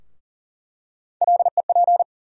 Morse code, heard as a keyed tone, about 40 wpm, 700 Hz. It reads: LEP
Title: USA Call Sign Postfixes - 3 Letters 40wpm